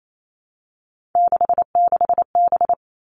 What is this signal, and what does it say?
Morse code, heard as a keyed tone, about 28 wpm, 700 Hz. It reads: 66B
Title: All Cut Numbers - 3-Digit 28wpm